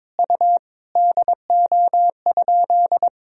Morse code, heard as a keyed tone, about 21 wpm, 700 Hz. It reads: U DO?
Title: Sentences from Top 100 Words and Abbreviations 22wpm